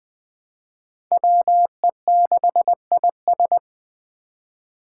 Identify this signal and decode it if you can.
Morse code, heard as a keyed tone, about 20 wpm, 700 Hz. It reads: WE6IS
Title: Five Challenging Letter-Number Combinations 20wpm